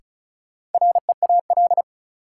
Morse code, heard as a keyed tone, about 35 wpm, 700 Hz. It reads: REAL
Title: Sentences from Top 300 Words - New Word Drill - 35wpm